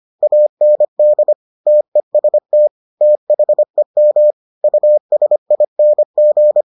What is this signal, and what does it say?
Morse code, heard as a keyed tone, about 25 wpm, 600 Hz. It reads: AND TEST THEM USING